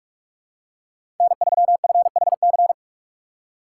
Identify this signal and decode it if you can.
Morse code, heard as a keyed tone, about 46 wpm, 700 Hz. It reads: N3FHC
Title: USA Call Signs - 5 Characters - 3x Character Spacing - 45wpm